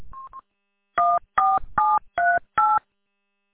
DTMF, heard as telephone touch tones, tones of about 230 ms. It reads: 14738
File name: w8cmn- -combined